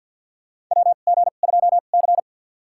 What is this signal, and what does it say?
Morse code, heard as a keyed tone, about 49 wpm, spaced as for 35 wpm, 700 Hz. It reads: WG2C
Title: USA Call Signs - 2x Character Spacing - 50wpm